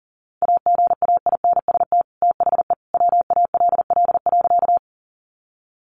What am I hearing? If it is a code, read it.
Morse code, heard as a keyed tone, about 40 wpm, 700 Hz. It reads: AGAINST THE WALL.
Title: Sentences from Top 1100 Words 40wpm